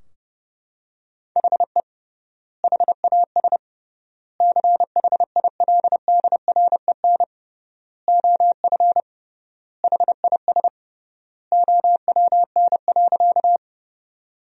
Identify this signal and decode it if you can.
Morse code, heard as a keyed tone, about 31 wpm, 700 Hz. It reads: HE HAS CHILDREN OF HIS OWN.
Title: 3x Word-Spacing - Sentences from Top 300 Words - 30wpm